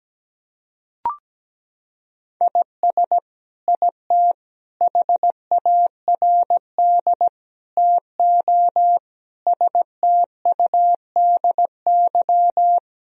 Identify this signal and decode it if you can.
Morse code, heard as a keyed tone, about 17 wpm, 700 Hz. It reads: IS IT HARD TO STUDY
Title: Sentences from Top 1000 Words - Review Part 3 - 17wpm